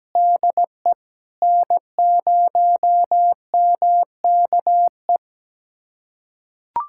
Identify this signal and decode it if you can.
Morse code, heard as a keyed tone, about 17 wpm, 700 Hz. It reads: DE N0MKE
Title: USA Callsign DE Callsign - 5 Characters - 17wpm